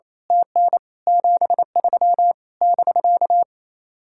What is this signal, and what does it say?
Morse code, heard as a keyed tone, about 27 wpm, 700 Hz. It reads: TN 73 <BK>
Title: Speed-Racing Common QSO Phrases Encore - 28wpm